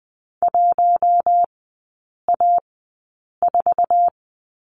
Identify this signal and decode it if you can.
Morse code, heard as a keyed tone, about 20 wpm, spaced as for 8 wpm, 700 Hz. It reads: 1A4
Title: 053 Lesson 10e - Letter r - Three Characters 20wpm